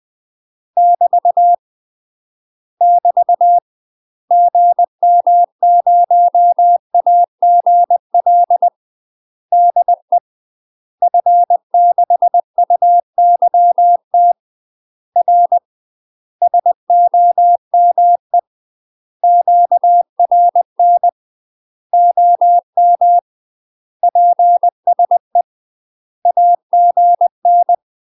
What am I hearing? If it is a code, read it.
Morse code, heard as a keyed tone, about 20 wpm, 700 Hz. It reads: <BT> <BT> GM0AGL DE F6UYT R SOME QRN OM PSE AGN
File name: UK MASTER QSOs TRAINING / 20WPM / UK MASTER QSOs 0051